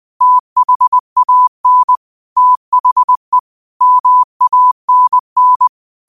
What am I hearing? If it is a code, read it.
Morse code, heard as a keyed tone, about 20 wpm, 1000 Hz. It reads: THAN THE MANN
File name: 20wpm